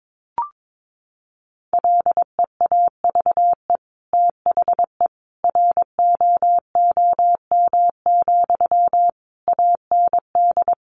Morse code, heard as a keyed tone, about 22 wpm, 700 Hz. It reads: LEAVE THE ROOM, AND
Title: Sentences from Top 1500 Words 22wpm